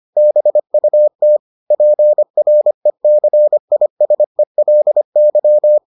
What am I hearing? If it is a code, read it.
Morse code, heard as a keyed tone, about 25 wpm, 600 Hz. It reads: BUT PRECISELY